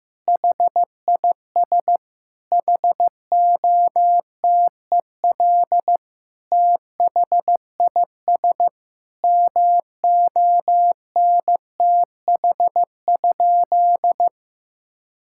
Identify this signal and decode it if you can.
Morse code, heard as a keyed tone, about 15 wpm, 700 Hz. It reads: HIS HOTEL THIS MONTH?
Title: Sentences from Top 1600 Words 15wpm